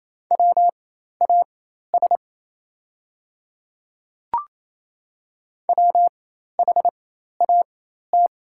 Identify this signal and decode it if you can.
Morse code, heard as a keyed tone, about 28 wpm, spaced as for 13 wpm, 700 Hz. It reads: WAS WHAT
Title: 4x Character Spacing - Top 100 Words - 28wpm